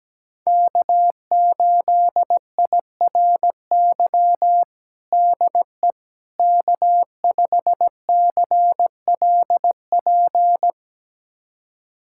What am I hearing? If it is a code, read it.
Morse code, heard as a keyed tone, about 17 wpm, 700 Hz. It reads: K8IRY DE K5CLP
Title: USA Callsign DE Callsign - 5 Characters - 17wpm